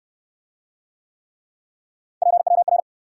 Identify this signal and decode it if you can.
Morse code, heard as a keyed tone, about 64 wpm, 700 Hz. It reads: PWR